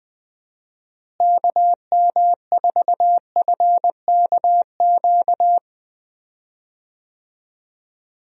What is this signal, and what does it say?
Morse code, heard as a keyed tone, about 21 wpm, 700 Hz. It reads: KM4FKQ
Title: USA Call Signs - 6 Characters - Repeated 2x - 20wpm